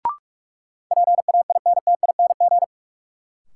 Morse code, heard as a keyed tone, about 46 wpm, 700 Hz. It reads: PAINTING
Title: Sentences from Top 1000 Words - Review Part 6 - New Word Drill - 45wpm